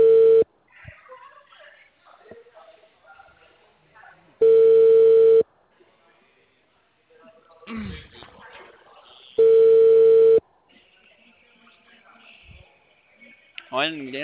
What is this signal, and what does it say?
Ringback tone (China, 450 Hz), with speech.